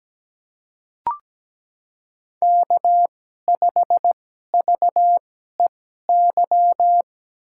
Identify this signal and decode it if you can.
Morse code, heard as a keyed tone, about 17 wpm, spaced as for 12 wpm, 700 Hz. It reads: K5VEY
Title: USA Call Signs - 5 Characters - 2x Character Spacing - 17wpm